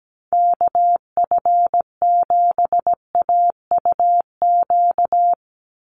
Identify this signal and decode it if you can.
Morse code, heard as a keyed tone, about 17 wpm, 700 Hz. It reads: KF7AUQ
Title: USA Call Signs 17wpm